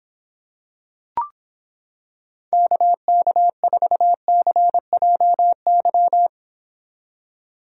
Morse code, heard as a keyed tone, about 26 wpm, 700 Hz. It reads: KK4CJY